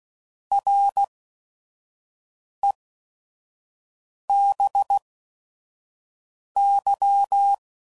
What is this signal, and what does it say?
Morse code, heard as a keyed tone, about 16 wpm, 800 Hz. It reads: R E B Y